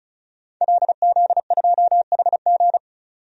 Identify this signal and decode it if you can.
Morse code, heard as a keyed tone, about 36 wpm, 700 Hz. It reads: LZ2HG
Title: DXCC-100 Call Signs - 5 Characters - 35wpm